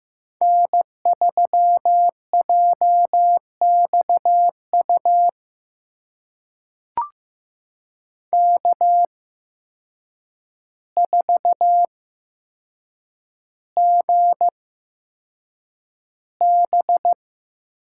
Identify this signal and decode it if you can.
Morse code, heard as a keyed tone, about 15 wpm, spaced as for 7 wpm, 700 Hz. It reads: N3JXU K4GB
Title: USA Call Signs - 8x Character Spacing - 15wpm